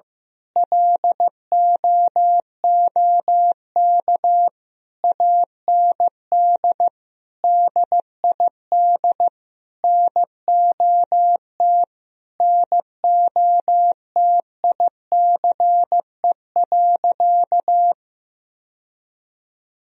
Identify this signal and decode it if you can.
Morse code, heard as a keyed tone, about 16 wpm, 700 Hz. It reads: LOOK AND DID NOT NOTICE.
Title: Sentences from Top 1000 Words - Review Part 2 - 15wpm